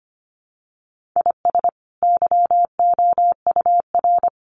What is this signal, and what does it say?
Morse code, heard as a keyed tone, about 25 wpm, 700 Hz. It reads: IS YOUR
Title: Sentences from Top 1500 Words 25wpm